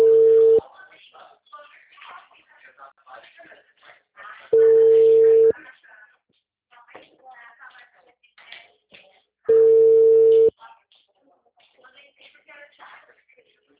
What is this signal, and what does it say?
Ringback tone (China, 450 Hz), with speech.